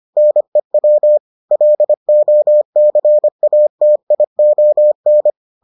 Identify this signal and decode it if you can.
Morse code, heard as a keyed tone, about 25 wpm, 600 Hz. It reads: NEW LOCATION